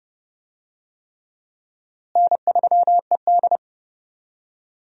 Morse code, heard as a keyed tone, about 32 wpm, 700 Hz. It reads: N3ED
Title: USA Call Signs - 4 Characters - Repeated 3x - 30wpm